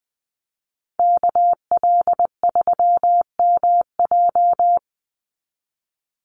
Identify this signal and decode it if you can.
Morse code, heard as a keyed tone, about 20 wpm, 700 Hz. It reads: KL3MJ
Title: USA Call Signs 20wpm